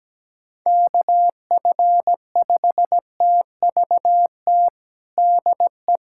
Morse code, heard as a keyed tone, about 17 wpm, 700 Hz. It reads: KF5TVT DE
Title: USA Callsign DE Callsign - 6 Characters - 17wpm